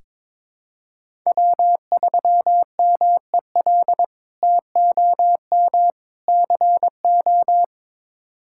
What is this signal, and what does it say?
Morse code, heard as a keyed tone, about 22 wpm, 700 Hz. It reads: W3MEL TOM CO
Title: K1USN SST QSO Practice - 22wpm